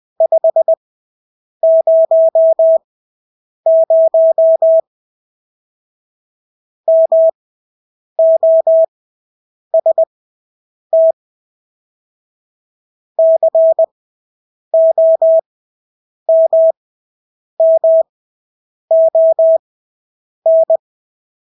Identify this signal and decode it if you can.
Morse code, heard as a keyed tone, about 20 wpm, spaced as for 8 wpm, 650 Hz. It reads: 500 MOST COMMON
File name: Engelske ord 20/8 wpm